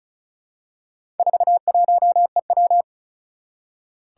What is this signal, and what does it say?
Morse code, heard as a keyed tone, about 36 wpm, 700 Hz. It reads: 41EW
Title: Four Challenging Letter-Number Combinations 35wpm